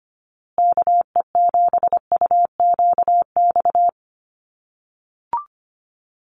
Morse code, heard as a keyed tone, about 25 wpm, 700 Hz. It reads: KE7UQX
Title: USA Call Signs Repeated 2x 25wpm